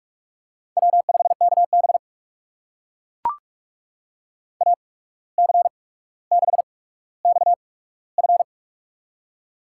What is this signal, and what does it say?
Morse code, heard as a keyed tone, about 44 wpm, spaced as for 18 wpm, 700 Hz. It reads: W5KB AC6XF
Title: USA Call Signs - 8x Character Spacing - 45wpm